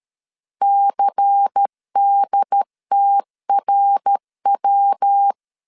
Morse code, heard as a keyed tone, about 13 wpm, 800 Hz. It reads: CDTRW
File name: LEZIONI / 6 PKX0 / 6 P 12wpm 2033